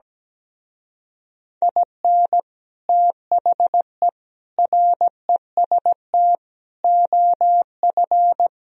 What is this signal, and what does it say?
Morse code, heard as a keyed tone, about 17 wpm, 700 Hz. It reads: IN THE REST OF
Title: Speed-Racing - Sets of 5 Words 17wpm